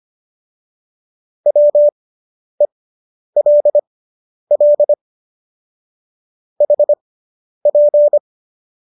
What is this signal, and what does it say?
Morse code, heard as a keyed tone, about 25 wpm, spaced as for 10 wpm, 590 Hz. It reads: WELL HP